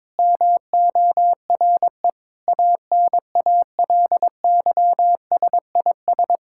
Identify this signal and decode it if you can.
Morse code, heard as a keyed tone, about 22 wpm, 700 Hz. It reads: MORE ANALYSIS
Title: Sentences from Top 1000 Words - Review Part 5 - 22wpm